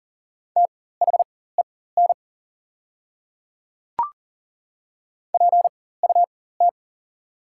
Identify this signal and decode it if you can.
Morse code, heard as a keyed tone, about 40 wpm, spaced as for 19 wpm, 700 Hz. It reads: THEN PUT
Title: 4x Character Spacing - Top 300 Words - 40wpm